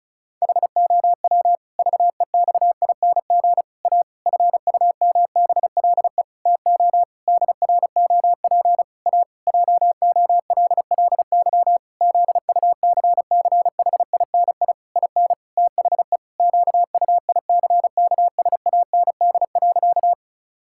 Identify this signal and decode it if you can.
Morse code, heard as a keyed tone, about 36 wpm, 700 Hz. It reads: HOW VEXING A FUMBLE TO DROP A JOLLY ZUCCHINI IN THE QUICKSAND.
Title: Pangram Challenge v2 - 35wpm